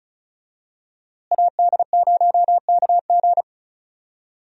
Morse code, heard as a keyed tone, about 34 wpm, 700 Hz. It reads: AD0KG
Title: USA Call Signs - 5 Characters - Repeated 3x - 35wpm